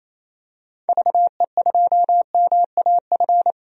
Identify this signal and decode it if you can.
Morse code, heard as a keyed tone, about 28 wpm, 700 Hz. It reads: VE2MAF
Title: DXCC-100 Call Signs - 6 Characters - Repeated 3x - 28wpm